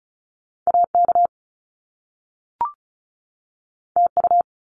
Morse code, heard as a keyed tone, about 36 wpm, 700 Hz. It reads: AK TU
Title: Top Two Letter Combinations in Words 35wpm